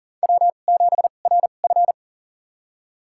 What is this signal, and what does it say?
Morse code, heard as a keyed tone, about 40 wpm, spaced as for 29 wpm, 700 Hz. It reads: W7RF
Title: USA Call Signs - 2x Character Spacing - 40wpm